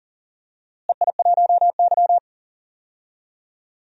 Morse code, heard as a keyed tone, about 40 wpm, 700 Hz. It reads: EI1Y
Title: DXCC-100 Call Signs - 4 Characters - Repeated 2x - 40wpm